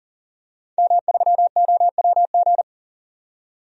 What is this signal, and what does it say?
Morse code, heard as a keyed tone, about 40 wpm, 700 Hz. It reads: M3OWG
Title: DXCC-100 Call Signs - 5 Characters - 40wpm